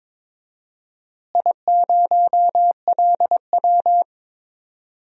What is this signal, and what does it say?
Morse code, heard as a keyed tone, about 22 wpm, 700 Hz. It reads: I0LW
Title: DXCC-100 Call Signs - 4 Characters - Repeated 3x - 22wpm